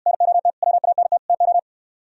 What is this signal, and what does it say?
Morse code, heard as a keyed tone, about 69 wpm, 700 Hz. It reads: NOT WANT TO